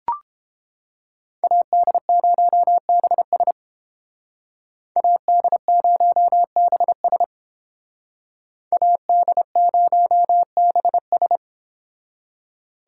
Morse code, heard as a keyed tone, about 29 wpm, 700 Hz. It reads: AD0BS AD0BS AD0BS